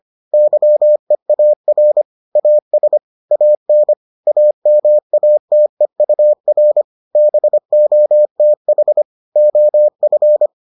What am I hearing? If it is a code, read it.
Morse code, heard as a keyed tone, about 25 wpm, 600 Hz. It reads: YEAR AS AN AMATEUR BOTH OF